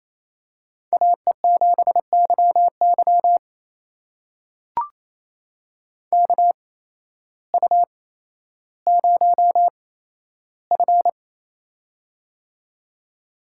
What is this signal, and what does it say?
Morse code, heard as a keyed tone, about 28 wpm, spaced as for 13 wpm, 700 Hz. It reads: AE7YY KU0F
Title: USA Call Signs - 8x Character Spacing - 28wpm